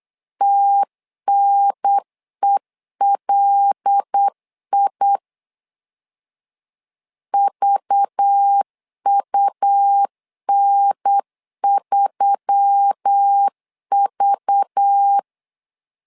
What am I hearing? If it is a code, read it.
Morse code, heard as a keyed tone, about 8 wpm, 800 Hz. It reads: TNELI VUN3V
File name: LEZIONI / 5 LBVC / 5 LBV 8wpm 2837